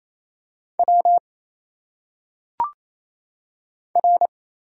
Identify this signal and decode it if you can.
Morse code, heard as a keyed tone, about 28 wpm, 700 Hz. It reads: W R
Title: Single Letters - Phonetic Pronunciation 28wpm